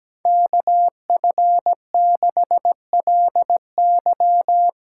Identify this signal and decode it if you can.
Morse code, heard as a keyed tone, about 17 wpm, 700 Hz. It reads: KF6LY
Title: USA Call Signs - 5 Characters - 4x Character Spacing - 17wpm